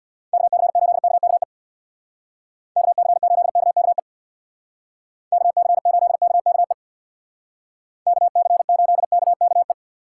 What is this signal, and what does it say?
Morse code, heard as a keyed tone, about 59 wpm, 700 Hz. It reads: KC8KKE KC8KKE KC8KKE KC8KKE
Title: Speed Racing - USA 2x3 Call Signs 50wpm